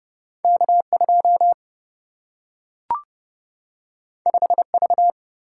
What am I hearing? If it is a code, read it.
Morse code, heard as a keyed tone, about 30 wpm, 700 Hz. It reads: K 2 5 V
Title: Single-Letter Single-Number 30wpm